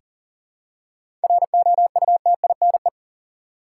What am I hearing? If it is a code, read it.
Morse code, heard as a keyed tone, about 37 wpm, 700 Hz. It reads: ROUTINE
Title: Seven Letter Words - 40wpm